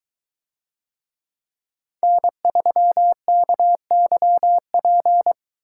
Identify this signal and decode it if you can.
Morse code, heard as a keyed tone, about 22 wpm, 700 Hz. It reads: N3KYP